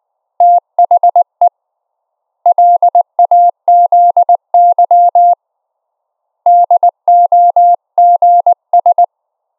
Morse code, heard as a keyed tone, about 19 wpm, 700 Hz. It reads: THE LAZY DOGS
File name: Op E 700 Hz